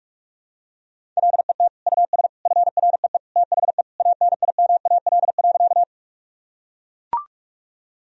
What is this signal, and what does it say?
Morse code, heard as a keyed tone, about 41 wpm, 700 Hz. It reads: LET US FREE THE ANIMAL.
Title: Sentences from Top 1000 Words - Review Part 1 - 45wpm